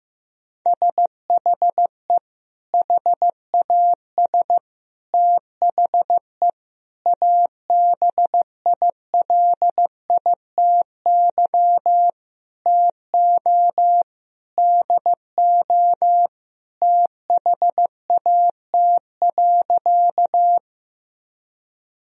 Morse code, heard as a keyed tone, about 15 wpm, 700 Hz. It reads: SHE HAS THE ABILITY TO DO THAT.
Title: Sentences from Top 1000 Words - Review Part 3 - 15wpm